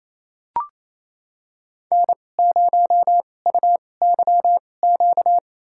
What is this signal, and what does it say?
Morse code, heard as a keyed tone, about 28 wpm, spaced as for 20 wpm, 700 Hz. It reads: N0UYQ
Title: USA Call Signs - 2x Character Spacing - 28wpm